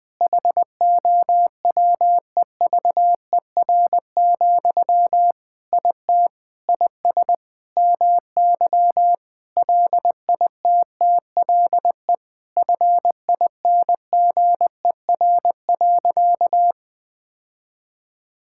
Morse code, heard as a keyed tone, about 20 wpm, 700 Hz. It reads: HOWEVER, IT IS MY LITTLE FINGER.
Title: Sentences from Top 1600 Words 20wpm